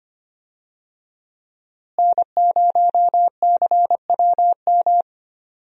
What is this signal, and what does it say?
Morse code, heard as a keyed tone, about 25 wpm, 700 Hz. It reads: N0CWM
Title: USA Call Signs - 5 Characters - Repeated 3x - 25wpm